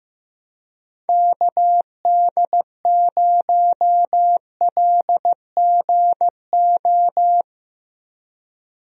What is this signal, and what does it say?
Morse code, heard as a keyed tone, about 16 wpm, 700 Hz. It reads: KD0LGO
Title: USA Call Signs - 6 Characters - 15wpm